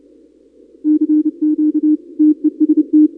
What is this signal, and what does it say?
Morse code, heard as a keyed tone, about 28 wpm, 310 Hz. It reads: CQ TEST
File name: contests / 2019-cw-wpx-kt5j-k5pi- -2330 0-31